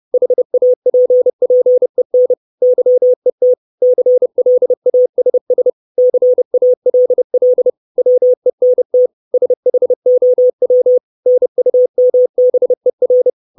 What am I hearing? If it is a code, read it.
Morse code, heard as a keyed tone, about 30 wpm, 500 Hz. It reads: HAPPEN YET CLASS CALL WENT SHOW NUMBER
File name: 632 Words 30WPM